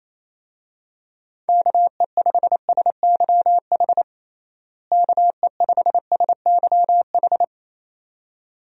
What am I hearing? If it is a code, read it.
Morse code, heard as a keyed tone, about 28 wpm, 700 Hz. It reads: KE5SYH KE5SYH
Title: USA Call Signs - 6 Characters - Repeated 2x - 28wpm